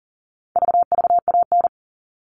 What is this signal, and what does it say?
Morse code, heard as a keyed tone, about 40 wpm, 700 Hz. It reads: VVAN
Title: All Cut Numbers - 4-Digit 40wpm